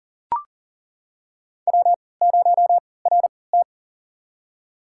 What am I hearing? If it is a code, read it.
Morse code, heard as a keyed tone, about 39 wpm, spaced as for 22 wpm, 700 Hz. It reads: W0RT
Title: USA Call Signs - 4 Characters - 3x Character Spacing - 40wpm